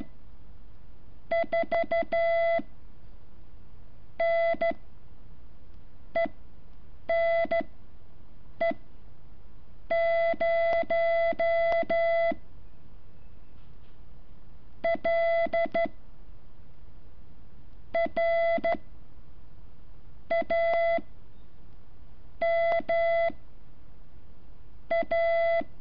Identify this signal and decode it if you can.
Morse code, heard as a keyed tone, about 13 wpm, 670 Hz. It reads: 4 N E N E 0 L R A M A
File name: curso / ejercicio 11